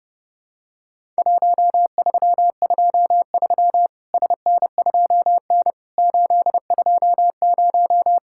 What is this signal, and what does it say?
Morse code, heard as a keyed tone, about 30 wpm, 700 Hz. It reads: 1323 SN2N 820
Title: Worked All Europe DX Contest - QTC Exchange - 30wpm